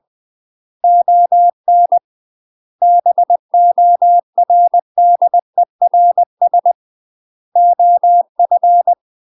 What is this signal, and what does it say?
Morse code, heard as a keyed tone, about 20 wpm, 700 Hz. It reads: ON BORDERS OF
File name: UK MASTER QSOs TRAINING / 20WPM / UK MASTER QSOs 0035